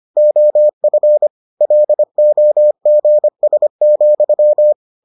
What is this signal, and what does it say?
Morse code, heard as a keyed tone, about 25 wpm, 600 Hz. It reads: OF LOGS,